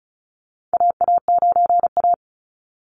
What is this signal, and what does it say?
Morse code, heard as a keyed tone, about 36 wpm, 700 Hz. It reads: AA9A
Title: Call Signs - 35wpm